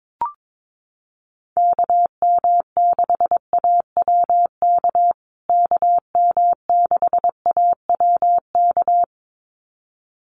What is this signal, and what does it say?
Morse code, heard as a keyed tone, about 22 wpm, 700 Hz. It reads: KM6AWK KM6AWK
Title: USA Call Signs Repeated 2x 22wpm